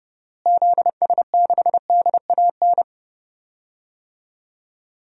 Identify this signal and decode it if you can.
Morse code, heard as a keyed tone, about 30 wpm, 700 Hz. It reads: ZS6DAN
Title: DXCC-100 Call Signs - 6 Characters - Repeated 3x - 30wpm